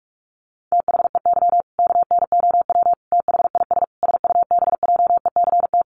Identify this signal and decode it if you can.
Morse code, heard as a keyed tone, about 44 wpm, 700 Hz. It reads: THEY KNOW THIS SUBJECT
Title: Sentences from Top 800 Words 45wpm